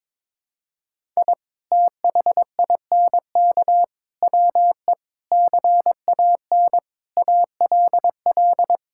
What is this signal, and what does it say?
Morse code, heard as a keyed tone, about 21 wpm, 700 Hz. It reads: I THINK WE CAN ALL
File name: Sentences with 7-Word Phrases - 22wpm